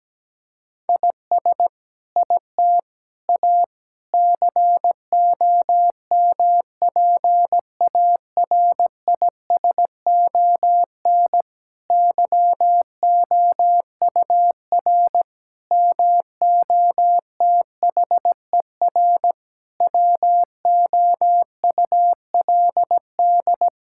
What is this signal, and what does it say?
Morse code, heard as a keyed tone, about 17 wpm, 700 Hz. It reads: IS IT A COMPARISON YOUR MOTHER WOULD
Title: Sentences from Top 1600 Words 17wpm